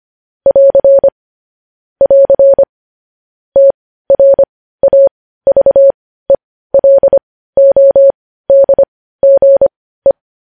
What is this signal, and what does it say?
Morse code, heard as a keyed tone, about 25 wpm, spaced as for 15 wpm, 550 Hz. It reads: <AR> <AR> TRAVELODGE
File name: podcast / News headlines-1x-25x15